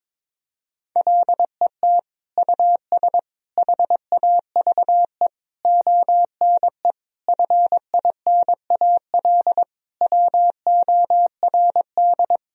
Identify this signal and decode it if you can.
Morse code, heard as a keyed tone, about 21 wpm, 700 Hz. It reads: LET US HAVE ONE FINAL WORD
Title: Sentences from Top 1000 Words - Review Part 2 - 22wpm